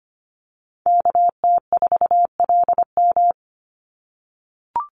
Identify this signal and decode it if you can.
Morse code, heard as a keyed tone, about 25 wpm, 700 Hz. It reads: KT4LM
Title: USA Call Signs 25wpm